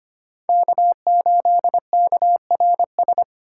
Morse code, heard as a keyed tone, about 25 wpm, 700 Hz. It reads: K8KRS
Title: USA Call Signs - 5 Characters - 3x Character Spacing - 25wpm